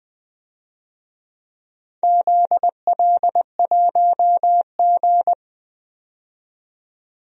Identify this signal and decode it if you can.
Morse code, heard as a keyed tone, about 20 wpm, 700 Hz. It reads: ZL1G
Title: DXCC-100 Call Signs - 4 Characters - 20wpm